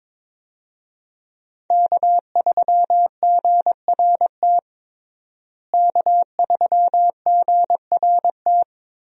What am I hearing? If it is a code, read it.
Morse code, heard as a keyed tone, about 22 wpm, 700 Hz. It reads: K3GRT K3GRT
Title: USA Call Signs - 5 Characters - Repeated 3x - 22wpm